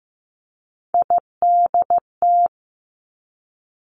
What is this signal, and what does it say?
Morse code, heard as a keyed tone, about 15 wpm, 700 Hz. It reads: IDT
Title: Three Challenging Letter-Number Combinations 15wpm